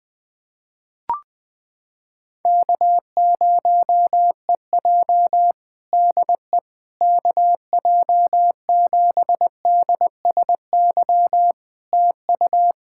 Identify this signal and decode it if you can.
Morse code, heard as a keyed tone, about 21 wpm, 700 Hz. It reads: K0EJ DE KJ7DSY TU
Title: Practice QSO - Exchange 4 - Ending - 20wpm